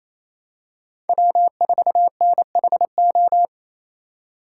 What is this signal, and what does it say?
Morse code, heard as a keyed tone, about 28 wpm, 700 Hz. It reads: W4NHO
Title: USA Call Signs - 8x Character Spacing - 28wpm